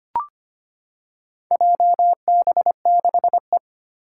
Morse code, heard as a keyed tone, about 25 wpm, 700 Hz. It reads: JB6E
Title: Four Challenging Letter-Number Combinations 25wpm